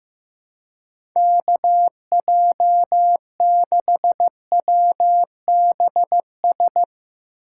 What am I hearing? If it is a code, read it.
Morse code, heard as a keyed tone, about 15 wpm, 700 Hz. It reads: KJ6WBS
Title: USA Call Signs - 6 Characters - 2x Character Spacing - 15wpm